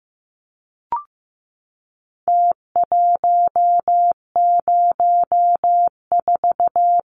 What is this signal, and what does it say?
Morse code, heard as a keyed tone, about 15 wpm, 700 Hz. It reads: T104
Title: Numbers - Contest Style - 4-Digit 15wpm